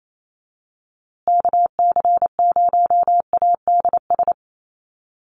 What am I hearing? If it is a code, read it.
Morse code, heard as a keyed tone, about 28 wpm, 700 Hz. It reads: KC0ADS
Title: USA Call Signs Repeated 2x 28wpm